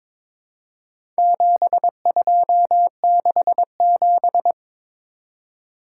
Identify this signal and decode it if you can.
Morse code, heard as a keyed tone, about 22 wpm, 700 Hz. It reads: 7267
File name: practice / 22wpm / Four Letter-Number Combinations - 22wpm